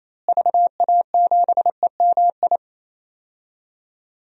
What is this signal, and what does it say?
Morse code, heard as a keyed tone, about 28 wpm, 700 Hz. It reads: VA7EMI
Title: DXCC-100 Call Signs - 6 Characters - Repeated 2x - 28wpm